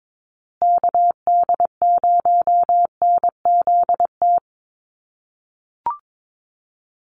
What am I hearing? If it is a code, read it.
Morse code, heard as a keyed tone, about 22 wpm, 700 Hz. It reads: KD0NZT
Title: USA Call Signs Repeated 3x 22wpm